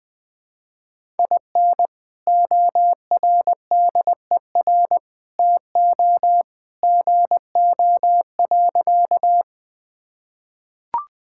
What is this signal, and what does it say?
Morse code, heard as a keyed tone, about 21 wpm, 700 Hz. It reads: IN ORDER TO GO.
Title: Sentences with 2-Word Phrases - 20wpm